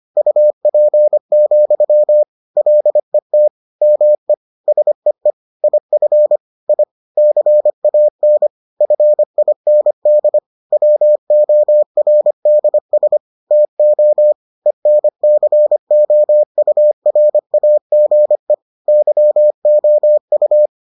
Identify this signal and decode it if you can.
Morse code, heard as a keyed tone, about 25 wpm, 600 Hz. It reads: UP, LET ME SEE IF I CAN FIND WORDS TO ENCOURAGE YOU